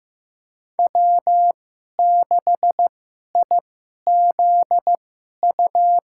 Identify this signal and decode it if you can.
Morse code, heard as a keyed tone, about 16 wpm, spaced as for 11 wpm, 700 Hz. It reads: W6IZU
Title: USA Call Signs - 5 Characters - 2x Character Spacing - 15wpm